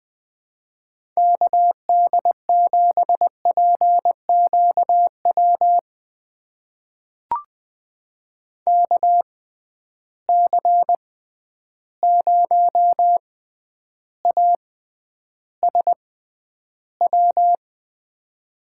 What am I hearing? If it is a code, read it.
Morse code, heard as a keyed tone, about 20 wpm, spaced as for 10 wpm, 700 Hz. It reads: KD7PQW KC0ASW
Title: USA Call Signs - 6 Characters - 6x Character Spacing - 20wpm